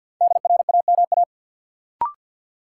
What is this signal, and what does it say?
Morse code, heard as a keyed tone, about 51 wpm, 700 Hz. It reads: DRAMA
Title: Sentences from Top 1500 Words - New Word Drill Alt - 50wpm